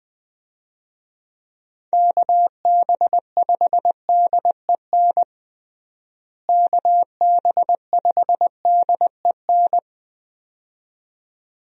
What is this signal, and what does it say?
Morse code, heard as a keyed tone, about 20 wpm, 700 Hz. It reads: KB5DEN KB5DEN
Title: USA Call Signs - 6 Characters - Repeated 2x - 20wpm